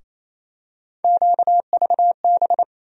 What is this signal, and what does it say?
Morse code, heard as a keyed tone, about 28 wpm, 700 Hz. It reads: QVB
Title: USA Call Sign Postfixes - 3 Letters 28wpm